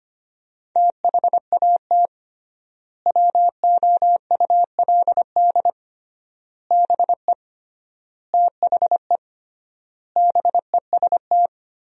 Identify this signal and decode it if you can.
Morse code, heard as a keyed tone, about 25 wpm, 700 Hz. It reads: THAT WOULD BE THE BEST
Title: 3x Word-Spacing - Sentences from Top 400 Words - 25wpm